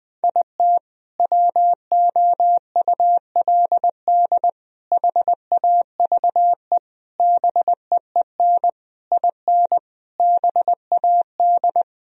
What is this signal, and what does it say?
Morse code, heard as a keyed tone, about 20 wpm, 700 Hz. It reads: IT WOULD HAVE BEEN IN BAD
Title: Sentences from Top 1000 Words - Review Part 1 - 20wpm